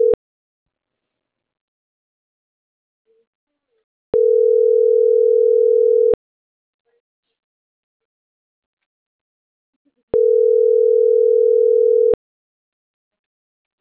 Ringback tone (North America), 440 + 480 Hz.